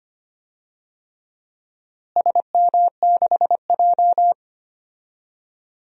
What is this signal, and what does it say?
Morse code, heard as a keyed tone, about 25 wpm, 700 Hz. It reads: SM6J
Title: DXCC-100 Call Signs - 4 Characters - Repeated 3x - 25wpm